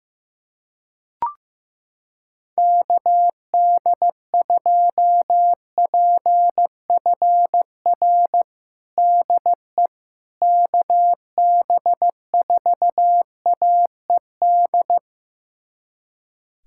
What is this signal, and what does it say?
Morse code, heard as a keyed tone, about 15 wpm, 700 Hz. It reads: KD2PFR DE KB4AED
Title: USA Callsign DE Callsign - 6 Characters - 15wpm